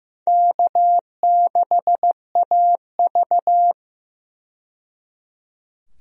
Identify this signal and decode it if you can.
Morse code, heard as a keyed tone, about 15 wpm, 700 Hz. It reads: K6AV
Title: USA Call Signs - 4 Characters - 15wpm